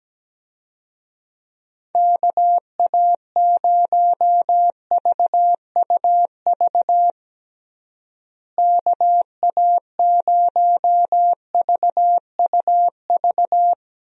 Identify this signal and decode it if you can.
Morse code, heard as a keyed tone, about 17 wpm, 700 Hz. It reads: KA0VUV KA0VUV
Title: USA Call Signs - 6 Characters - Repeated 2x - 17wpm